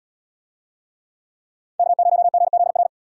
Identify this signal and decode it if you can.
Morse code, heard as a keyed tone, about 77 wpm, 700 Hz. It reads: K0GZR